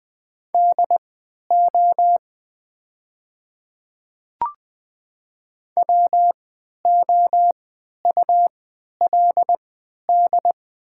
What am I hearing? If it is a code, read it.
Morse code, heard as a keyed tone, about 20 wpm, spaced as for 11 wpm, 690 Hz. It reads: DO WOULD
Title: 3x Character Spacing - Top 100 Words - 20wpm